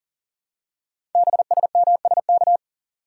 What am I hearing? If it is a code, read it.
Morse code, heard as a keyed tone, about 39 wpm, 700 Hz. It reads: BSMSK
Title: Code Groups - 5 Letters - 40wpm